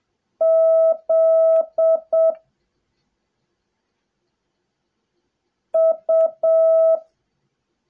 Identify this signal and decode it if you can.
Morse code, heard as a keyed tone, about 7 wpm, 660 Hz. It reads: Z U
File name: jh8cbh / 2021cw / cw08a